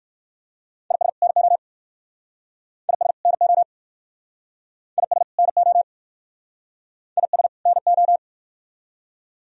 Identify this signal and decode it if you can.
Morse code, heard as a keyed tone, about 62 wpm, 700 Hz. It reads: IS NO IS NO IS NO IS NO